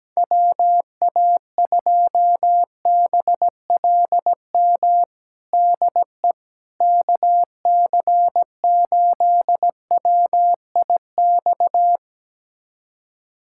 Morse code, heard as a keyed tone, about 17 wpm, 700 Hz. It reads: WA2BLM DE KC8WIX
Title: USA Callsign DE Callsign 17wpm